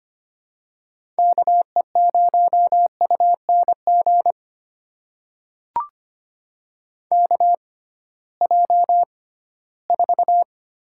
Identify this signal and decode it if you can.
Morse code, heard as a keyed tone, about 25 wpm, spaced as for 16 wpm, 700 Hz. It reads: KE0UNG KJ4
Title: USA Call Signs - 6 Characters - 6x Character Spacing - 25wpm